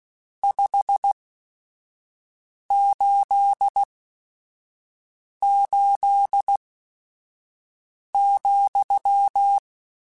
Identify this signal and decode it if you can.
Morse code, heard as a keyed tone, about 16 wpm, spaced as for 5 wpm, 800 Hz. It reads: 588,
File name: RSTs / RST-mp3s / Your RSTs @5